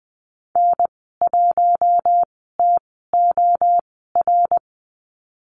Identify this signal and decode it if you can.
Morse code, heard as a keyed tone, about 19 wpm, spaced as for 14 wpm, 700 Hz. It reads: N1TOR
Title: 082 Lesson 14g - Number 2 - Call Signs 20wpm